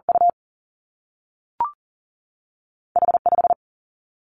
Morse code, heard as a keyed tone, about 42 wpm, 700 Hz. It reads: U H 5
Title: Two Challenging Letter-Number Combinations 40wpm